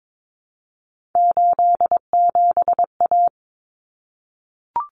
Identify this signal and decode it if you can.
Morse code, heard as a keyed tone, about 22 wpm, 700 Hz. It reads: 87A
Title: Common Cut Numbers - 3-Digit 22wpm